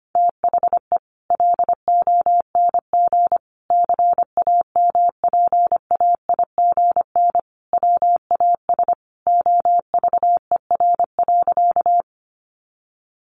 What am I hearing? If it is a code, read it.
Morse code, heard as a keyed tone, about 25 wpm, 700 Hz. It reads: THE LONG CAMPAIGN WAS OVER.
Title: Sentences from Top 1200 Words 25wpm